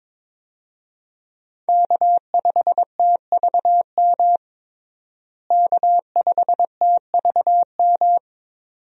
Morse code, heard as a keyed tone, about 21 wpm, 700 Hz. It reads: K5TVM K5TVM
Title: USA Call Signs - 5 Characters - Repeated 2x - 22wpm